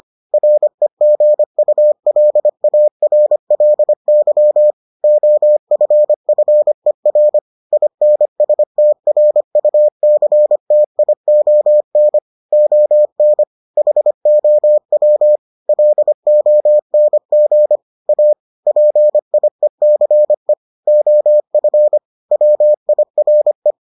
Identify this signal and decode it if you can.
Morse code, heard as a keyed tone, about 25 wpm, 600 Hz. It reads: REGULARLY OFFER INSTRUCTION ON HOW LONG A PIECE OF WIRE